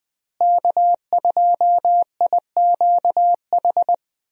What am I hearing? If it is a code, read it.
Morse code, heard as a keyed tone, about 20 wpm, 700 Hz. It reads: K2IQH
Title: USA Call Signs - 5 Characters - Repeated 3x - 20wpm